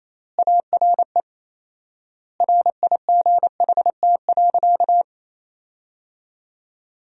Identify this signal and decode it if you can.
Morse code, heard as a keyed tone, about 28 wpm, 700 Hz. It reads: ARE RIGHT.
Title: 4x Word-Spacing - Sentences from Top 500 Words - 28wpm